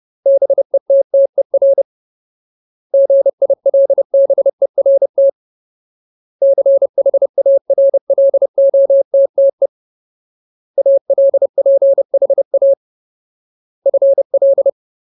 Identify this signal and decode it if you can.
Morse code, heard as a keyed tone, about 30 wpm, 550 Hz. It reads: BETTER GILBERT CHARLOTTE ALPHA FL
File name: Ultimate Word List / 30WPM Ultimate